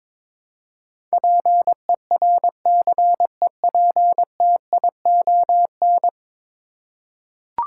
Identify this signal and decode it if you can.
Morse code, heard as a keyed tone, about 22 wpm, 700 Hz. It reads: PERCEPTION
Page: morse-code-ninja-practice-22wpm